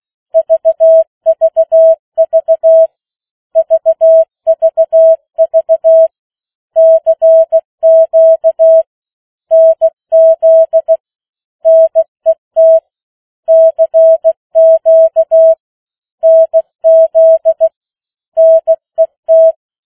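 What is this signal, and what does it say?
Morse code, heard as a keyed tone, about 16 wpm, 650 Hz. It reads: VVV VVV CQ NZ NET CQ NZ NET
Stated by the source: Opening of the New Zealand national amateur radio net at 15 wpm, which meets on 3535.0 kHz.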